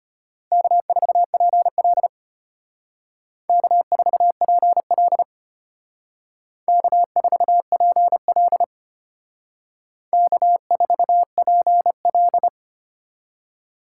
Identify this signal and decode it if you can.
Morse code, heard as a keyed tone, about 32 wpm, 700 Hz. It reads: K4PL K4PL K4PL K4PL